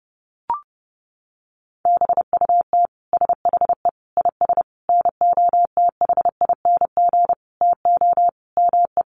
Morse code, heard as a keyed tone, about 30 wpm, 700 Hz. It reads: BUT SHE IS NOTHING TO ME
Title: Sentences from Top 500 Words 30wpm